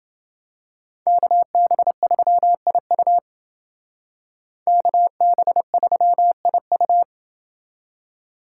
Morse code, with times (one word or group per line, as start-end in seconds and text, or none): 1.1-3.2 KB3IU
4.7-7.0 KB3IU